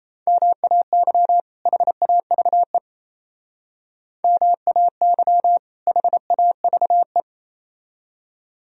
Morse code, heard as a keyed tone, about 36 wpm, 700 Hz. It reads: MAY HAVE MAY HAVE